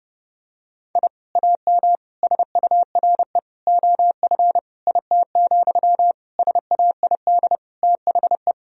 Morse code, heard as a keyed tone, about 32 wpm, 700 Hz. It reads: I AM SURE OF IT, SAID THE
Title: Sentences from Top 1000 Words - Review Part 4 - 30wpm